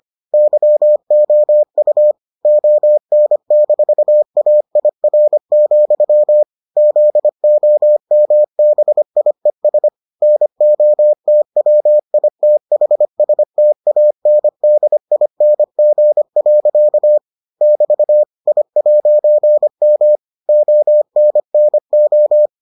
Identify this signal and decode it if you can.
Morse code, heard as a keyed tone, about 25 wpm, 600 Hz. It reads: YOU ON-AIR, ZOMBIES NOTWITHSTANDING. <BT> I'M ONNO